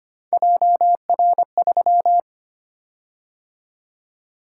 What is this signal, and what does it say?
Morse code, heard as a keyed tone, about 25 wpm, 700 Hz. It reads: JR3
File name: Three Letter-Number Combinations - 25wpm